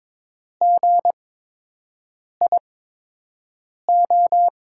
Morse code, heard as a keyed tone, about 22 wpm, spaced as for 6 wpm, 700 Hz. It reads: GIO
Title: USA Call Signs - 5 Characters - 8x Character Spacing - 22wpm